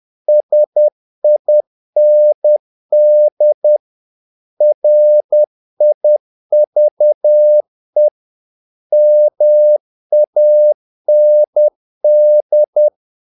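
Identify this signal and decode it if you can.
Morse code, heard as a keyed tone, about 10 wpm, 600 Hz. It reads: SIND RIVE MAND
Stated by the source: Morse practice of short Danish words at 10 wpm, from CW med Gnister.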